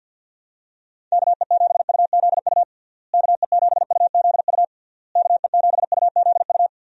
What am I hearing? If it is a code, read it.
Morse code, heard as a keyed tone, about 52 wpm, 700 Hz. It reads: KE7UZU KE7UZU KE7UZU
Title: USA Call Signs - 6 Characters - Repeated 3x - 50wpm